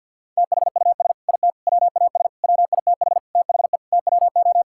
Morse code, heard as a keyed tone, about 52 wpm, 700 Hz. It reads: THUS IT WAS WITH THE TWO
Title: Sentences from Top 1000 Words - Review Part 5 - 50wpm